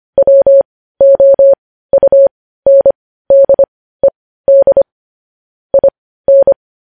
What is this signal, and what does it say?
Morse code, heard as a keyed tone, about 25 wpm, spaced as for 15 wpm, 550 Hz. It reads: WOUNDED IN
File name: podcast / News headlines-1x-25x15